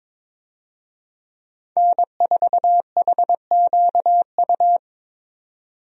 Morse code, heard as a keyed tone, about 22 wpm, 700 Hz. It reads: N4HQU
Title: USA Call Signs - 5 Characters - 22wpm